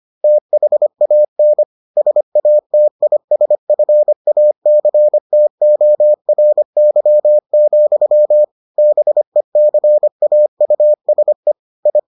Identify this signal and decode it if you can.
Morse code, heard as a keyed tone, about 25 wpm, 600 Hz. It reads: THAN SATISFACTORY, BECAUSE I